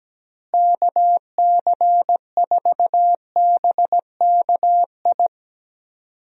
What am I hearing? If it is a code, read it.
Morse code, heard as a keyed tone, about 17 wpm, 700 Hz. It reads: KC4BKI
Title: USA Call Signs - 6 Characters - Repeated 3x - 17wpm